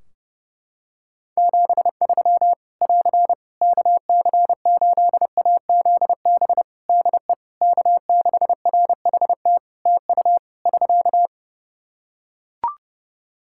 Morse code, heard as a keyed tone, about 30 wpm, 700 Hz. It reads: 73 <AR> KC8AZB DE K6RHT TU <SK>
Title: Practice QSO - Exchange 4 - Ending - 30wpm